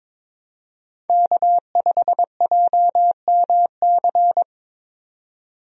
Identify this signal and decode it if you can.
Morse code, heard as a keyed tone, about 22 wpm, 700 Hz. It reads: K5JMC
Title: USA Call Signs - 5 Characters - 22wpm